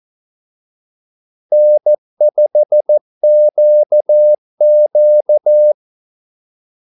Morse code, heard as a keyed tone, about 14 wpm, 600 Hz. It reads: N5QQ
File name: Callsigns-14wpm